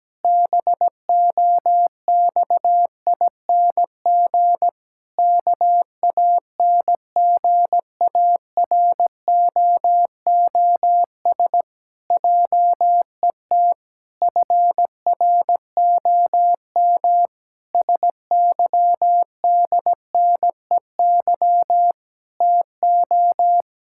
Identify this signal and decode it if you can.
Morse code, heard as a keyed tone, about 17 wpm, 700 Hz. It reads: BOXING KANGAROOS JET FROM SYDNEY TO
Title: Pangram Challenge v2 - 17wpm